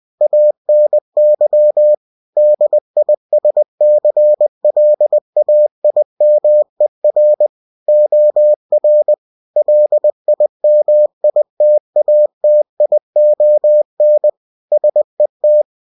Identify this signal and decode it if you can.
Morse code, heard as a keyed tone, about 20 wpm, 600 Hz. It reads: ANY DISCLAIMER OR LIMITATION SET
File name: cw-books / winnie-the-pooh / winnie-20wpm-page-0119